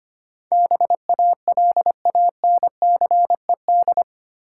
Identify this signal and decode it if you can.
Morse code, heard as a keyed tone, about 25 wpm, 700 Hz. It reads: BALANCED
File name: Eight Letter Words - 25wpm